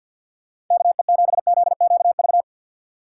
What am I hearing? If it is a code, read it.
Morse code, heard as a keyed tone, about 51 wpm, 700 Hz. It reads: KE7ZQV
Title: USA Call Signs - 3x Character Spacing - 50wpm